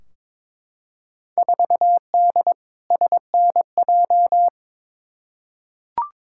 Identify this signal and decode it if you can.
Morse code, heard as a keyed tone, about 22 wpm, 700 Hz. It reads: 4D SNJ
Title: Field Day Exchanges Encore - 22wpm